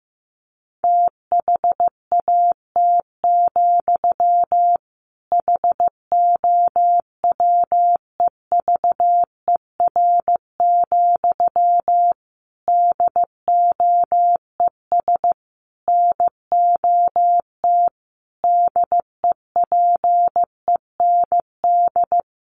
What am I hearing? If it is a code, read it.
Morse code, heard as a keyed tone, about 15 wpm, 700 Hz. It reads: THAT, HOWEVER, DOES NOT DEPEND
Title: Sentences from Top 1300 Words 15wpm